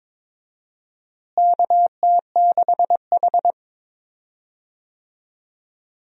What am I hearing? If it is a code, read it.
Morse code, heard as a keyed tone, about 22 wpm, 700 Hz. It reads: KT6H
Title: USA Call Signs - 4 Characters - Repeated 3x - 22wpm